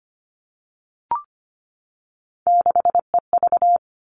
Morse code, heard as a keyed tone, about 25 wpm, 700 Hz. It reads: 6EV
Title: All Cut Numbers - 3-Digit 25wpm